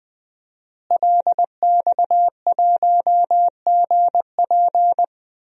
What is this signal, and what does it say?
Morse code, heard as a keyed tone, about 19 wpm, 700 Hz. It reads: LX1GP
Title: DXCC-100 Call Signs - 5 Characters - Repeated 2x - 20wpm